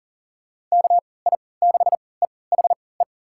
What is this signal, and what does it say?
Morse code, heard as a keyed tone, about 40 wpm, spaced as for 23 wpm, 700 Hz. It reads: KI6EHE
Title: USA Call Signs - 6 Characters - 3x Character Spacing - 40wpm